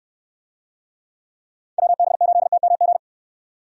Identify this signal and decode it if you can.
Morse code, heard as a keyed tone, about 69 wpm, 700 Hz. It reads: WB8TMG